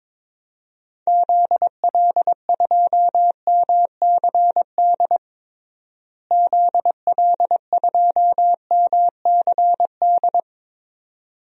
Morse code, heard as a keyed tone, about 22 wpm, 700 Hz. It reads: ZL2MCD ZL2MCD
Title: DXCC-100 Call Signs - 6 Characters - Repeated 2x - 22wpm